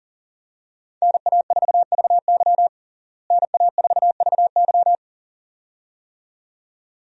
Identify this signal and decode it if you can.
Morse code, heard as a keyed tone, about 37 wpm, 700 Hz. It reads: NA4VY NA4VY
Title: USA Call Signs - 5 Characters - Repeated 2x - 40wpm